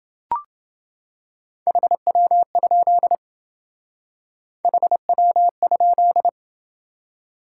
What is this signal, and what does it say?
Morse code, heard as a keyed tone, about 27 wpm, 700 Hz. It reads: HW? HW?